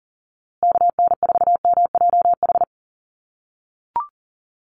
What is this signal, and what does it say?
Morse code, heard as a keyed tone, about 40 wpm, 700 Hz. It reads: KN4MJH
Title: USA Call Signs Repeated 2x 40wpm